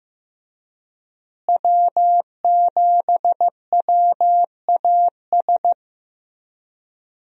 Morse code, heard as a keyed tone, about 15 wpm, 700 Hz. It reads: W7WAS
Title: USA Call Signs - 5 Characters - Repeated 3x - 15wpm